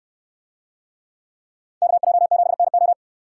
Morse code, heard as a keyed tone, about 69 wpm, 700 Hz. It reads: KJ7NO